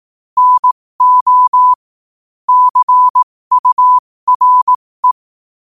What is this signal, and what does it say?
Morse code, heard as a keyed tone, about 18 wpm, spaced as for 15 wpm, 1000 Hz. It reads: NO CURE
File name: qotdfiles / 15wpm